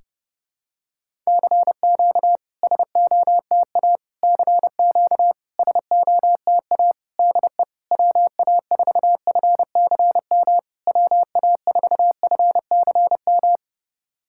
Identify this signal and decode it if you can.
Morse code, heard as a keyed tone, about 30 wpm, 700 Hz. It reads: CQ SOTA CQ SOTA DE WA4FCM WA4FCM
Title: SOTA QSO Practice 30wpm